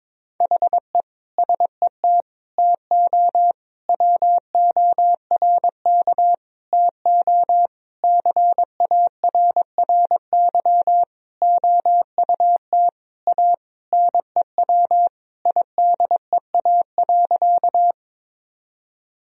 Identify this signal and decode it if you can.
Morse code, heard as a keyed tone, about 22 wpm, 700 Hz. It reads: HE SET TO WORK TO CARRY OUT A NEW IDEA.
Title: Sentences from Top 1000 Words - Review Part 1 - 22wpm